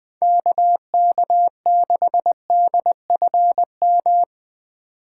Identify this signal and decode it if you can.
Morse code, heard as a keyed tone, about 20 wpm, 700 Hz. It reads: KK6DFM